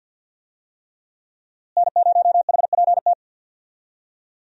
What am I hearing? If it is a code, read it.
Morse code, heard as a keyed tone, about 52 wpm, 700 Hz. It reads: N0HPT
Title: USA Call Signs - 5 Characters - Repeated 3x - 50wpm